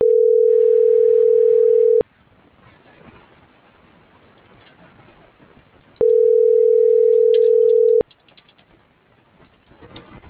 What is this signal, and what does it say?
Ringback tone (North America), 440 + 480 Hz.